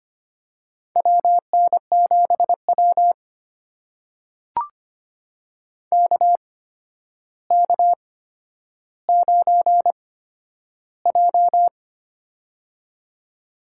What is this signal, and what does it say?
Morse code, heard as a keyed tone, about 25 wpm, spaced as for 11 wpm, 700 Hz. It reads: WN7W KK9J
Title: USA Call Signs - 4 Characters - 8x Character Spacing - 25wpm